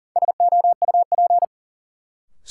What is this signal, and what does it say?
Morse code, heard as a keyed tone, about 40 wpm, 700 Hz. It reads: SOUP
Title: Sentences from Top 1000 Words - Review Part 6 - New Word Drill - 40wpm